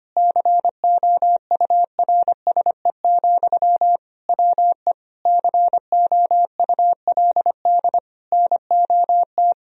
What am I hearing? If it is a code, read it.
Morse code, heard as a keyed tone, about 25 wpm, 700 Hz. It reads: COURSE, WE COULD NOT
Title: Sentences from Top 1000 Words - Review Part 2 - 25wpm